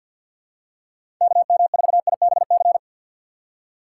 Morse code, heard as a keyed tone, about 47 wpm, 700 Hz. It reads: KM4IBC
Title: USA Call Signs - 6 Characters - 4x Character Spacing - 50wpm